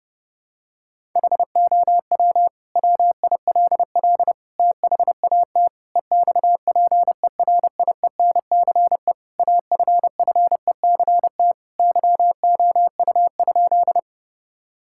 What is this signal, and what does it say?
Morse code, heard as a keyed tone, about 31 wpm, 700 Hz. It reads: HOW WILL THAT EXPERIENCE AFFECT YOU?
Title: Sentences from Top 1000 Words - Review Part 6 - 30wpm